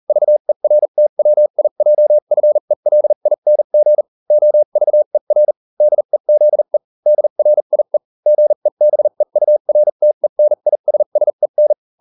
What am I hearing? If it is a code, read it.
Morse code, heard as a keyed tone, about 40 wpm, spaced as for 35 wpm, 600 Hz. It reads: VERTWIJFELING OVER DEZE DRIE GEBEURTENISSEN